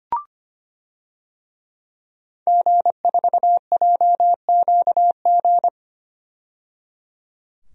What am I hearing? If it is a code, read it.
Morse code, heard as a keyed tone, about 25 wpm, 700 Hz. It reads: G4JQG
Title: DXCC-100 Call Signs - 5 Characters - 25wpm